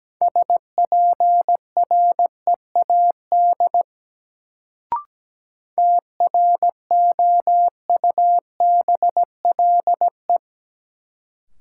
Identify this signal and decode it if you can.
Morse code, heard as a keyed tone, about 17 wpm, 700 Hz. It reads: SPREAD TROUBLE
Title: Sentences from Top 1200 Words - New Word Drill Alt - 17wpm